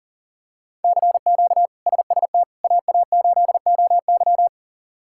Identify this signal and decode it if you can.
Morse code, heard as a keyed tone, about 40 wpm, 700 Hz. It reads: CQ SST AA8OY
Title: K1USN SST QSO Practice - 40wpm